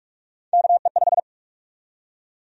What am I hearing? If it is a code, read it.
Morse code, heard as a keyed tone, about 45 wpm, 700 Hz. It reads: KE5
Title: USA Call Sign Prefixes - Repeated 2x - 45wpm